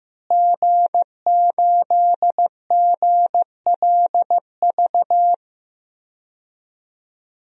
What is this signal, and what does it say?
Morse code, heard as a keyed tone, about 15 wpm, 700 Hz. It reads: G8GLV
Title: DXCC-100 Call Signs - 5 Characters - Repeated 3x - 15wpm